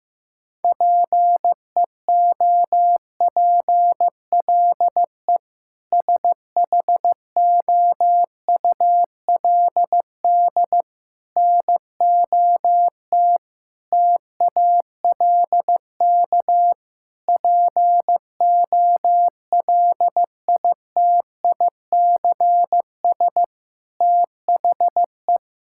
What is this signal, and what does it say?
Morse code, heard as a keyed tone, about 15 wpm, 700 Hz. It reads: PEOPLE SHOULD NOT TALK POLITICS THE
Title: Sentences from Top 1000 Words - Review Part 6 - 15wpm